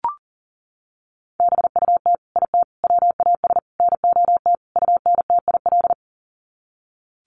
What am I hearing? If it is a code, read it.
Morse code, heard as a keyed tone, about 40 wpm, 700 Hz. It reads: BUT IT WAS NOT UNTIL
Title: Sets of 5 Words Encore2 40wpm